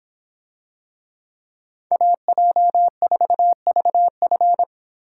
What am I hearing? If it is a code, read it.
Morse code, heard as a keyed tone, about 26 wpm, 700 Hz. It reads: AJ4VF